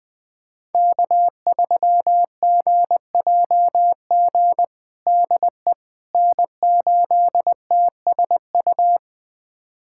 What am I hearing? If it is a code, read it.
Morse code, heard as a keyed tone, about 19 wpm, 700 Hz. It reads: K3GJG DE N8TSU
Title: USA Callsign DE Callsign - 5 Characters - 20wpm